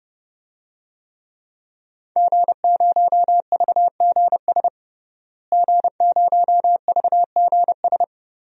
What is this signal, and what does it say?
Morse code, heard as a keyed tone, about 31 wpm, 700 Hz. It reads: G0VGS G0VGS
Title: DXCC-100 Call Signs - 5 Characters - Repeated 3x - 30wpm